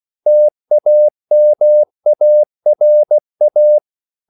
Morse code, heard as a keyed tone, about 16 wpm, 600 Hz. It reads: TAMARA